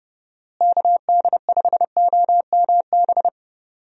Morse code, heard as a keyed tone, about 31 wpm, 700 Hz. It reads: KD5OMB
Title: USA Call Signs - 6 Characters - Repeated 3x - 30wpm